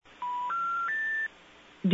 Special information tone.
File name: N4E-Due-To-An-Emergency-076